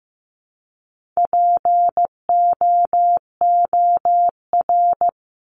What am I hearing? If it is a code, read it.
Morse code, heard as a keyed tone, about 15 wpm, 700 Hz. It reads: POOR
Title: Top 751-1000 Words 15wpm